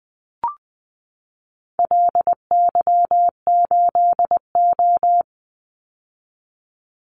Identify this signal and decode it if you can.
Morse code, heard as a keyed tone, about 20 wpm, 700 Hz. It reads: LY8O
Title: Call Signs - 20wpm